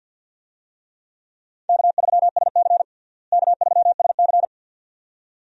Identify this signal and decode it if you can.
Morse code, heard as a keyed tone, about 52 wpm, 700 Hz. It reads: K3SC K3SC
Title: USA Call Signs - 4 Characters - Repeated 2x - 50wpm